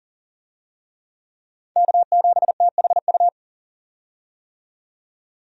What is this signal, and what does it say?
Morse code, heard as a keyed tone, about 40 wpm, 700 Hz. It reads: K7THU
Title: USA Call Signs - 5 Characters - 40wpm